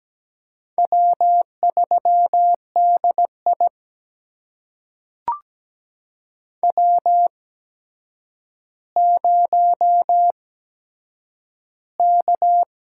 Morse code, heard as a keyed tone, about 17 wpm, spaced as for 8 wpm, 700 Hz. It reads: W3DI W0K
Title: USA Call Signs - 4 Characters - 8x Character Spacing - 17wpm